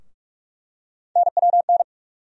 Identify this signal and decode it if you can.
Morse code, heard as a keyed tone, about 43 wpm, 700 Hz. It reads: NWN
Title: USA Call Sign Postfixes - 3 Letters 45wpm